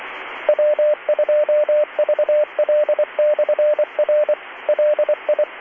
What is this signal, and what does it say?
Morse code, heard as a keyed tone, about 24 wpm, 600 Hz, with radio noise. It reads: W2VL/R LI